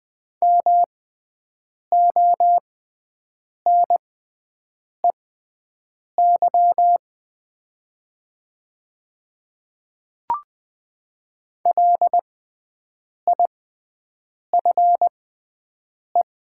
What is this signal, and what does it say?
Morse code, heard as a keyed tone, about 20 wpm, spaced as for 7 wpm, 700 Hz. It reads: MONEY LIFE
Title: 6x Character Spacing - Top 500 Words - 20wpm